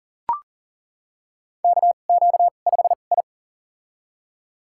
Morse code, heard as a keyed tone, about 38 wpm, spaced as for 28 wpm, 700 Hz. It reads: KQ5I
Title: USA Call Signs - 4 Characters - 2x Character Spacing - 40wpm